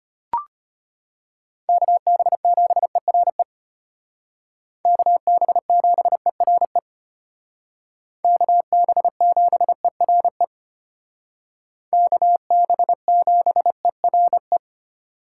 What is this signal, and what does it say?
Morse code, heard as a keyed tone, about 30 wpm, 700 Hz. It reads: KB7ERE KB7ERE KB7ERE KB7ERE
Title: Speed Racing - USA 2x3 Call Signs 25wpm